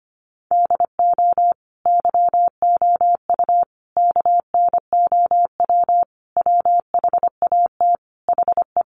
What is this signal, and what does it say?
Morse code, heard as a keyed tone, about 25 wpm, 700 Hz. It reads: DO YOU KNOW WHAT HE
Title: Sentences from Top 300 Words 25wpm